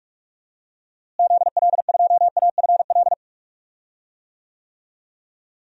Morse code, heard as a keyed tone, about 46 wpm, 700 Hz. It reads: ZL2AFL
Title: DXCC-100 Call Signs - 6 Characters - Repeated 2x - 45wpm